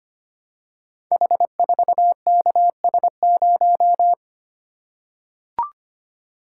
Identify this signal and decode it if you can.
Morse code, heard as a keyed tone, about 25 wpm, 700 Hz. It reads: H4KS0
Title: Code Groups - 5 Letter:Numbers - 25wpm